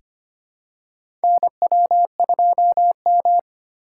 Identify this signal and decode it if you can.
Morse code, heard as a keyed tone, about 25 wpm, 700 Hz. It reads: NW2M
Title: USA Call Signs - 4 Characters - 4x Character Spacing - 25wpm